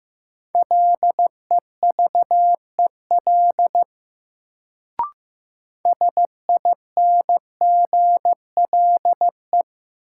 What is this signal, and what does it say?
Morse code, heard as a keyed tone, about 15 wpm, 700 Hz. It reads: LEVEL SINGLE
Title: Sentences from Top 700 Words - New Word Drill - 15wpm